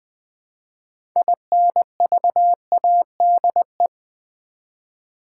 Morse code, heard as a keyed tone, about 20 wpm, 700 Hz. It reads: INVADE
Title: Build A Word - 2 Letters At A Time - 6-Letter Words - 20wpm